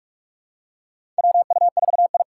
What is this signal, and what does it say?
Morse code, heard as a keyed tone, about 41 wpm, 700 Hz. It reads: WU4I
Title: USA Call Signs - 2x Character Spacing - 45wpm